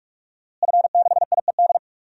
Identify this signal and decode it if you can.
Morse code, heard as a keyed tone, about 43 wpm, 700 Hz. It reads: F6IED
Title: DXCC-100 Call Signs - 5 Characters - Repeated 2x - 45wpm